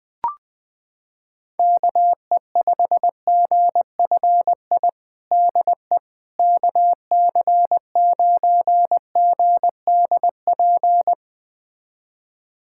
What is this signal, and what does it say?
Morse code, heard as a keyed tone, about 21 wpm, 700 Hz. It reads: KE5GFI DE KC9GDP
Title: USA Callsign DE Callsign 20wpm